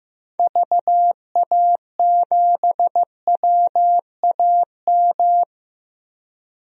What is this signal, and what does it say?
Morse code, heard as a keyed tone, about 16 wpm, 690 Hz. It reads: VA7WAM
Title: DXCC-100 Call Signs - 6 Characters - 15wpm